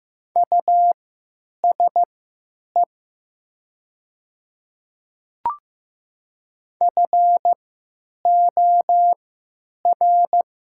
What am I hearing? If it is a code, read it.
Morse code, heard as a keyed tone, about 15 wpm, spaced as for 8 wpm, 700 Hz. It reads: USE FOR
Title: 3x Character Spacing - Top 100 Words - 15wpm